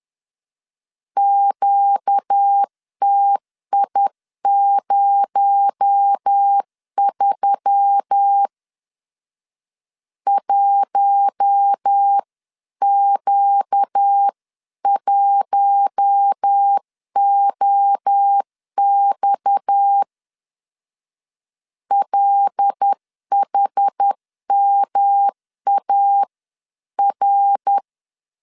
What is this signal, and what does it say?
Morse code, heard as a keyed tone, about 10 wpm, 800 Hz. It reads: QTI03 1Q1OX LHMAR
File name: CW / DET / 1 6 1500C S C 10wpm 4252